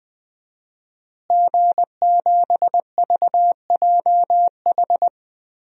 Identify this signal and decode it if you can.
Morse code, heard as a keyed tone, about 20 wpm, 700 Hz. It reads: G7VJH
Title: DXCC-100 Call Signs - 5 Characters - Repeated 3x - 20wpm